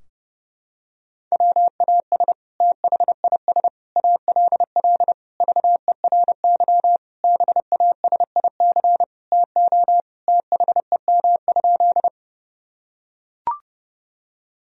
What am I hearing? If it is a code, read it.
Morse code, heard as a keyed tone, about 32 wpm, 700 Hz. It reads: WAS THIS ALL VERY BASIC TO THEM?
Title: Sentences from Top 1000 Words - Review Part 3 - 30wpm